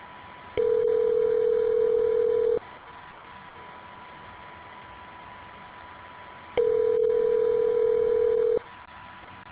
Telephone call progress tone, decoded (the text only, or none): ringback tone